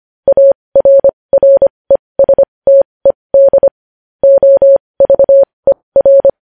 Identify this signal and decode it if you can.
Morse code, heard as a keyed tone, about 25 wpm, spaced as for 20 wpm, 550 Hz. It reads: ARRESTED OVER
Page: news headlines in morse code at 20 wpm